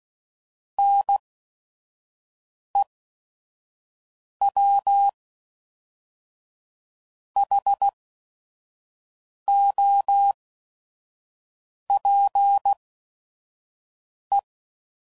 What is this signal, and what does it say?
Morse code, heard as a keyed tone, about 16 wpm, spaced as for 5 wpm, 800 Hz. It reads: NEWHOPE
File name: CW Code Groups @5WPM / fromSM's / Cities